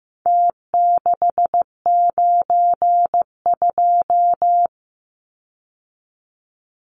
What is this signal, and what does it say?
Morse code, heard as a keyed tone, about 15 wpm, 700 Hz. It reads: T692
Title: Numbers - Contest Style - 4-Digit 15wpm